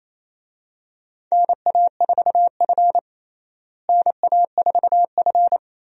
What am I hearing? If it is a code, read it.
Morse code, heard as a keyed tone, about 28 wpm, 700 Hz. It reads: NA4F NA4F
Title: USA Call Signs - 4 Characters - Repeated 2x - 28wpm